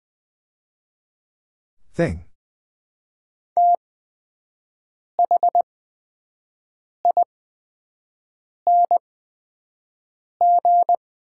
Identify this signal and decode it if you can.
Morse code, heard as a keyed tone, about 20 wpm, spaced as for 5 wpm, 700 Hz. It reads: ETHING
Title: 8x Character Spacing - Top 200 Words - 20wpm